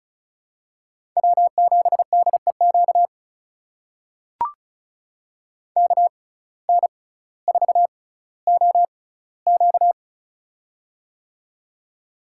Morse code, heard as a keyed tone, about 36 wpm, spaced as for 18 wpm, 700 Hz. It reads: W7DEQ KN4OQ
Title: USA Call Signs - 5 Characters - 6x Character Spacing - 35wpm